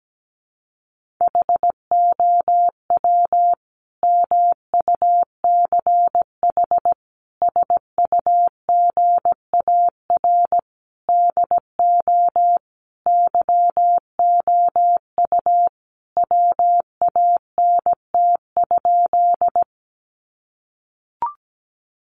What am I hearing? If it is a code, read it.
Morse code, heard as a keyed tone, about 17 wpm, 700 Hz. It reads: HOW MUCH SUGAR DO YOU WANT?
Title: Sentences from Top 1200 Words 17wpm